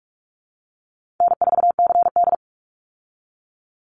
Morse code, heard as a keyed tone, about 44 wpm, 700 Hz. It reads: N4CD
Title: Easy Call Signs Repeated 2x - 45wpm